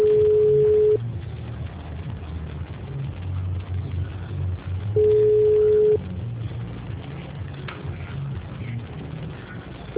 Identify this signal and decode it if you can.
Ringback tone (Europe, 425 Hz).
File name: out- -2007- - - .5411